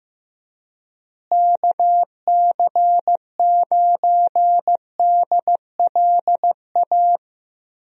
Morse code, heard as a keyed tone, about 15 wpm, 700 Hz. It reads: KC9DLA
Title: USA Call Signs - 6 Characters - Repeated 3x - 15wpm